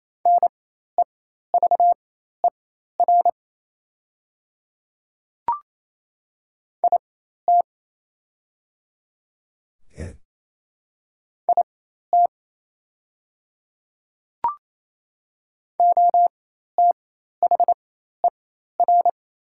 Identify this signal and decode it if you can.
Morse code, heard as a keyed tone, about 27 wpm, spaced as for 13 wpm, 700 Hz. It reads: NEVER IT IT OTHER
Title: 4x Character Spacing - Top 300 Words - 28wpm